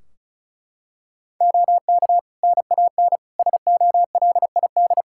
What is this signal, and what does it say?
Morse code, heard as a keyed tone, about 34 wpm, 700 Hz. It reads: OK NAN SOLID
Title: Practice QSO - Exchange 3 - More Info - 35wpm